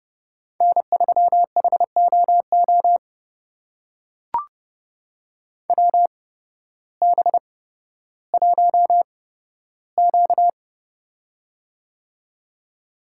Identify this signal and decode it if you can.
Morse code, heard as a keyed tone, about 30 wpm, spaced as for 14 wpm, 700 Hz. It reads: N3HOO WB1Q
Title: USA Call Signs - 8x Character Spacing - 30wpm